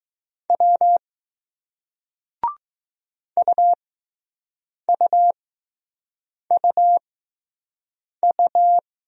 Morse code, heard as a keyed tone, about 20 wpm, spaced as for 6 wpm, 700 Hz. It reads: W UUUU